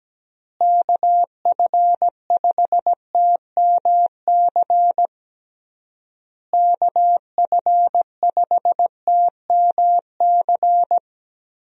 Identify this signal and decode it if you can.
Morse code, heard as a keyed tone, about 17 wpm, 700 Hz. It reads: KF5TMC KF5TMC
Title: USA Call Signs - 6 Characters - Repeated 3x - 17wpm